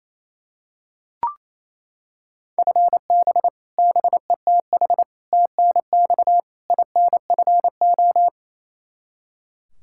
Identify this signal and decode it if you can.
Morse code, heard as a keyed tone, about 27 wpm, 700 Hz. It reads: FB BETH TNX INFO
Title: Practice QSO - Exchange 3 - More Info - 28wpm